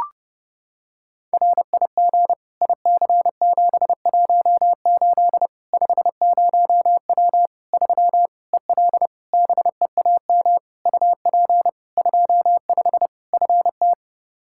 Morse code, heard as a keyed tone, about 32 wpm, 700 Hz. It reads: RIG IC718 50W 3 EL BEAM UP 25 FT
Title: Practice QSO - Exchange 2 - Info - 30wpm